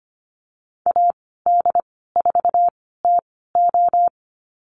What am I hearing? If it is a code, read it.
Morse code, heard as a keyed tone, about 25 wpm, spaced as for 16 wpm, 700 Hz. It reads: AD4TO
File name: Lesson 15g - Number 5 - Call Signs 25wpm